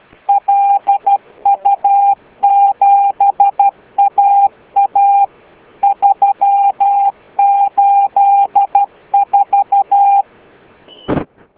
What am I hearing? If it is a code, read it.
Morse code, heard as a keyed tone, about 12 wpm, 800 Hz. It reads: LU7AA 384